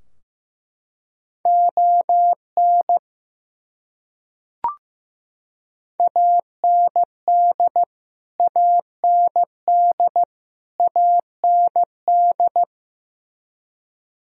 Morse code, heard as a keyed tone, about 16 wpm, 700 Hz. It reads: ON AND AND AND
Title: Top 100 Words Repeated 15wpm